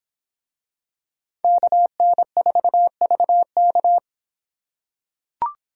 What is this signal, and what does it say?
Morse code, heard as a keyed tone, about 25 wpm, 700 Hz. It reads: KN4VK